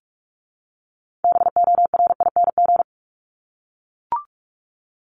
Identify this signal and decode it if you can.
Morse code, heard as a keyed tone, about 44 wpm, 700 Hz. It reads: BORING
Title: Top 1501-1750 Words 45wpm